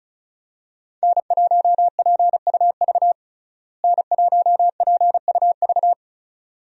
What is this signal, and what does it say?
Morse code, heard as a keyed tone, about 36 wpm, 700 Hz. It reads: N1PUV N1PUV
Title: USA Call Signs - 5 Characters - Repeated 2x - 35wpm